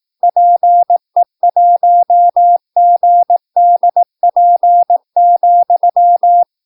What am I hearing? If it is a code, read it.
Morse code, heard as a keyed tone, about 18 wpm, 700 Hz. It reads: PE1GDP,
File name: PI4AA Uitzending